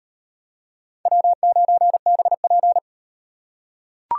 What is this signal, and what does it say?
Morse code, heard as a keyed tone, about 34 wpm, 700 Hz. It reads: W9BP